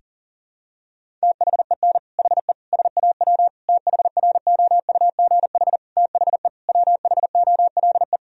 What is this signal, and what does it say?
Morse code, heard as a keyed tone, about 40 wpm, 700 Hz. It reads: THEN HE SAW THROUGH THE WHOLE
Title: Sentences from Top 1000 Words - Review Part 1 - 40wpm